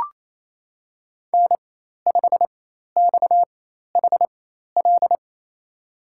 Morse code, heard as a keyed tone, about 28 wpm, spaced as for 13 wpm, 700 Hz. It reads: N5XHL
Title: USA Call Signs - 4x Character Spacing - 28wpm